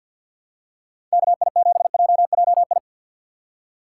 Morse code, heard as a keyed tone, about 51 wpm, 700 Hz. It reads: KI7JJI
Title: USA Call Signs - 6x Character Spacing - 50wpm